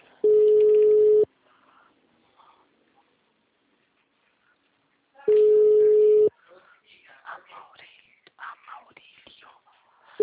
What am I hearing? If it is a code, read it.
Ringback tone (Europe, 425 Hz), with speech.
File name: out- -2021- - - .355